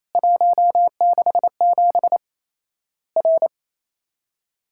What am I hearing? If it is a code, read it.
Morse code, heard as a keyed tone, about 28 wpm, 700 Hz. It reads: 167 R
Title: Worked All Europe DX Contest - QTC Exchange - 28wpm